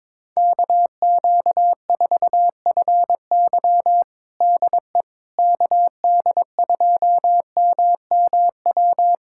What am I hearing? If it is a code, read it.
Morse code, heard as a keyed tone, about 22 wpm, 700 Hz. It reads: KQ4FY DE KD2MMW
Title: USA Callsign DE Callsign 22wpm